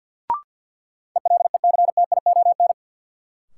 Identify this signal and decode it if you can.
Morse code, heard as a keyed tone, about 53 wpm, 700 Hz. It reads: ELECTION
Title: Sentences from Top 1500 Words - New Word Drill - 50wpm